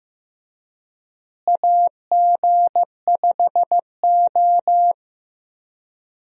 Morse code, heard as a keyed tone, about 15 wpm, 700 Hz. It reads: AG5O
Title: USA Call Signs - 4 Characters - Repeated 2x - 15wpm